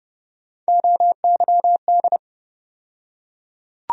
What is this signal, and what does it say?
Morse code, heard as a keyed tone, about 30 wpm, 700 Hz. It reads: OYD
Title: USA Call Sign Postfixes - 3 Letters 30wpm